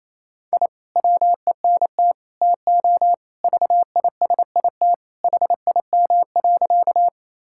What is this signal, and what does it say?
Morse code, heard as a keyed tone, about 28 wpm, 700 Hz. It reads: I WENT TO VISIT HIM.
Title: Sentences from Top 1000 Words - Review Part 6 - 28wpm